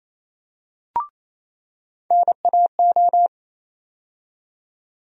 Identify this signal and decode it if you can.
Morse code, heard as a keyed tone, about 28 wpm, 700 Hz. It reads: NAO
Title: USA Call Sign Postfixes - 3 Letters 28wpm